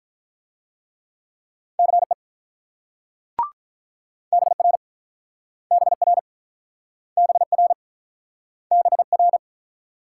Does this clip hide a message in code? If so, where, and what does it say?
Yes: Morse code from 1.8 to 9.4 s, heard as a keyed tone, about 48 wpm, 700 Hz. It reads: CE BR BR BR BR